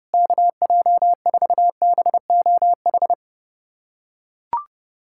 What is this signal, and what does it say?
Morse code, heard as a keyed tone, about 32 wpm, 700 Hz. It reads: KJ4BOH
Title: USA Callsign DE Callsign - 6 Characters - 30wpm